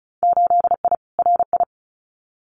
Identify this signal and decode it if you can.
Morse code, heard as a keyed tone, about 34 wpm, 700 Hz. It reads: 8I RI
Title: Winter Field Day Exchanges Encore 35wpm